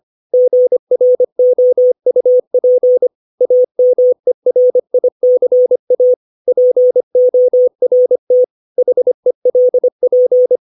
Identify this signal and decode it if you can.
Morse code, heard as a keyed tone, about 25 wpm, 500 Hz. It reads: GROUP AMERICA PORT HELP
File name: 632 words / 632 Words 25WPM